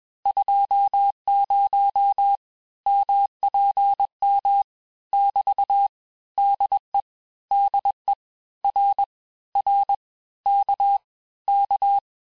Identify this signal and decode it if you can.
Morse code, heard as a keyed tone, about 21 wpm, 800 Hz. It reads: 20 MPM <BT> DE DE R R K K
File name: cours-du-lundi files / cours-1592--teste-abreviations-vitesse-15mpm-puis-20mpm